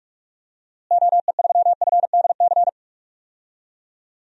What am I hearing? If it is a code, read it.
Morse code, heard as a keyed tone, about 46 wpm, 700 Hz. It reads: OE3FDC
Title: DXCC-100 Call Signs - 6 Characters - Repeated 2x - 45wpm